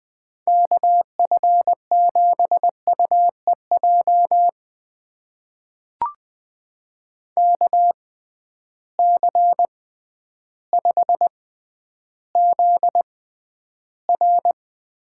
Morse code, heard as a keyed tone, about 20 wpm, spaced as for 11 wpm, 700 Hz. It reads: KF7UEJ KC5ZR
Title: USA Call Signs - 6x Character Spacing - 20wpm